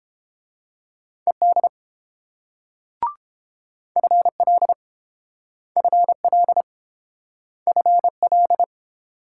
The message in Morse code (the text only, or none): ED FL FL FL